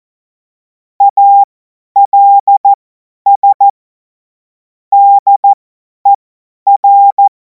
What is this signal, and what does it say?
Morse code, heard as a keyed tone, about 14 wpm, spaced as for 10 wpm, 800 Hz. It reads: ALS DER
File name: Morsekurs / Witze / Witz 28. April 2026 50 F70